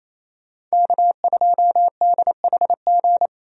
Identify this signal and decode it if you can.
Morse code, heard as a keyed tone, about 28 wpm, 700 Hz. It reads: K2DHG
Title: USA Call Signs - 5 Characters - 28wpm